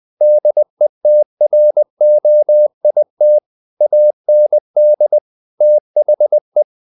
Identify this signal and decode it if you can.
Morse code, heard as a keyed tone, about 20 wpm, 600 Hz. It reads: DETROIT AND THE
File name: the old man and the sea / the old man and the sea part 1 20wpm